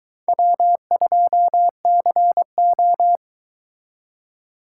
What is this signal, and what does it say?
Morse code, heard as a keyed tone, about 22 wpm, 700 Hz. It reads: W2CO